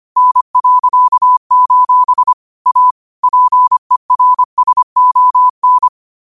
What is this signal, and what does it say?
Morse code, heard as a keyed tone, about 25 wpm, 1000 Hz. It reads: N.: A PERSON
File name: qotdfiles / 25wpm